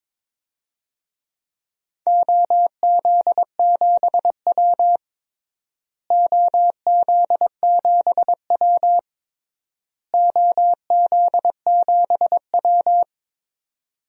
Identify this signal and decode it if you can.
Morse code, heard as a keyed tone, about 22 wpm, 700 Hz. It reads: OZ7W OZ7W OZ7W
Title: DXCC-100 Call Signs - 4 Characters - Repeated 3x - 22wpm